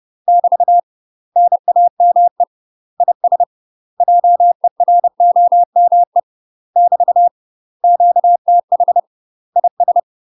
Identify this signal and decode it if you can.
Morse code, heard as a keyed tone, about 30 wpm, 700 Hz. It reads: <BT> NAME IS JEROME <BT> QTH IS
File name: UK MASTER QSOs TRAINING / 30WPM / UK MASTER QSOs 0004